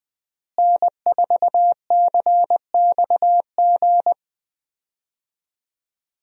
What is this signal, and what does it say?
Morse code, heard as a keyed tone, about 20 wpm, 700 Hz. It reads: N4CXG
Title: USA Call Signs - 5 Characters - Repeated 3x - 20wpm